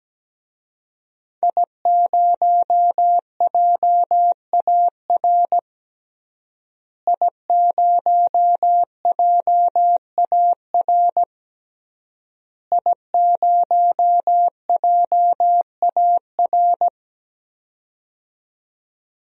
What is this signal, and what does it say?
Morse code, heard as a keyed tone, about 17 wpm, 700 Hz. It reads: I0JAR I0JAR I0JAR
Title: DXCC-100 Call Signs - 5 Characters - Repeated 3x - 17wpm